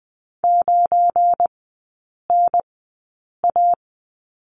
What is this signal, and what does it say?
Morse code, heard as a keyed tone, about 20 wpm, spaced as for 8 wpm, 700 Hz. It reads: 9NA
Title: 172 Lesson 28e - Number 9 - Three Characters 20wpm